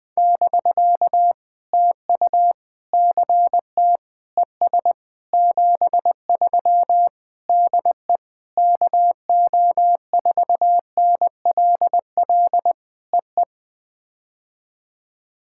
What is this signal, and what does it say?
Morse code, heard as a keyed tone, about 21 wpm, 700 Hz. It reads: <BK> TU CT ES 73 DE KO4NLL EE
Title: POTA QSO Practice - 20wpm